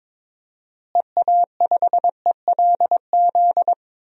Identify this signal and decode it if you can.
Morse code, heard as a keyed tone, about 22 wpm, 700 Hz. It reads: EA5ELZ
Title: DXCC-100 Call Signs - 6 Characters - Repeated 3x - 22wpm